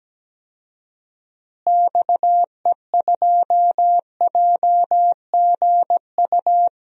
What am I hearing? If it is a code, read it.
Morse code, heard as a keyed tone, about 17 wpm, 700 Hz. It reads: XE2JGU
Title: DXCC-100 Call Signs - 6 Characters - Repeated 2x - 17wpm